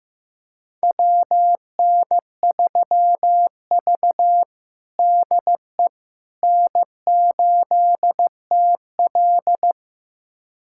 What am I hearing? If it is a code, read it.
Morse code, heard as a keyed tone, about 15 wpm, 700 Hz. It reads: WN3V DE N8TL
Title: USA Callsign DE Callsign - 4 Characters - 15wpm